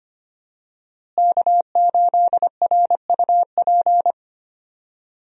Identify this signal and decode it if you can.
Morse code, heard as a keyed tone, about 25 wpm, 700 Hz. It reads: K8RUP
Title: USA Call Signs - 5 Characters - Repeated 3x - 25wpm